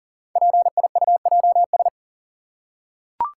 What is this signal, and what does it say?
Morse code, heard as a keyed tone, about 37 wpm, 690 Hz. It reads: PIUJS
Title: Code Groups - 5 Letters - 40wpm